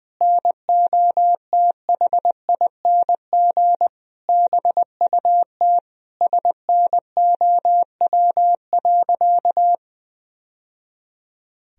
Morse code, heard as a keyed tone, about 21 wpm, 700 Hz. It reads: NOTHING BUT SNOW.
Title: Sentences from Top 1000 Words - Review Part 3 - 20wpm